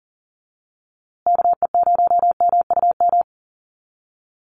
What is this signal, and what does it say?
Morse code, heard as a keyed tone, about 40 wpm, 700 Hz. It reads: KE0MUM
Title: USA Call Signs Repeated 3x 40wpm